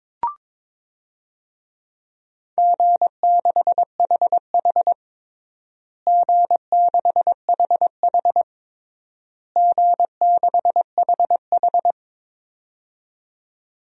Morse code, heard as a keyed tone, about 21 wpm, 700 Hz. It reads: G6HH G6HH G6HH
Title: DXCC-100 Call Signs - 4 Characters - Repeated 3x - 22wpm